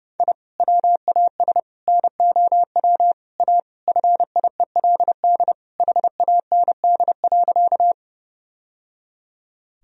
Morse code, heard as a keyed tone, about 32 wpm, 700 Hz. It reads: I WAS NOW A FIELD HAND.
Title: Sentences from Top 1000 Words - Review Part 2 - 30wpm